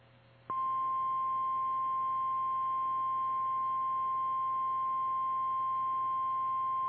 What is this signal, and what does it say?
Emergency Alert System (SAME) tones (the 1050 Hz attention tone). It is heard from 0.7 to 5.8 s.
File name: eas / KKPRKXPN / KKPRKXPN